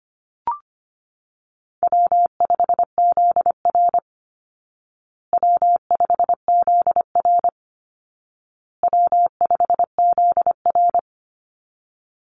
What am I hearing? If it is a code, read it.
Morse code, heard as a keyed tone, about 25 wpm, 700 Hz. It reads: W5ZR W5ZR W5ZR
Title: Easy Call Signs Repeated 3x - 25wpm